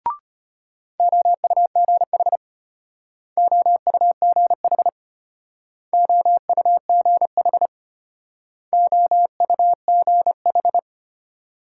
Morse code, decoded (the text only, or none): OUGH OUGH OUGH OUGH